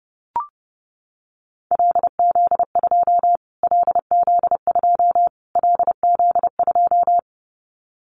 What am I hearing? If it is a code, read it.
Morse code, heard as a keyed tone, about 30 wpm, 700 Hz. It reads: LZ2 LZ2 LZ2
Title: Callsign Prefixes for Top 25 DXCC - Repeated 3x - 30wpm